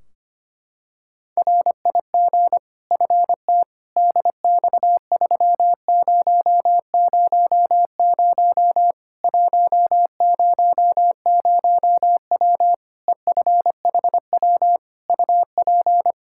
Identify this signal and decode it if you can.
Morse code, heard as a keyed tone, about 25 wpm, 700 Hz. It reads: RIG FT DX3000 100W EFHW UP
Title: K1USN SST QSO Practice 25wpm